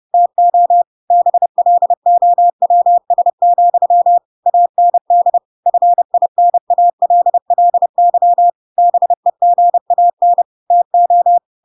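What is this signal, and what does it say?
Morse code, heard as a keyed tone, about 30 wpm, 700 Hz. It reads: TO BLOWS, AND FINALLY BEGAN TO
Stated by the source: Chapter 2 of The Adventures of Pinocchio read as a book in Morse code (CW) at 30 wpm, by ik1hge.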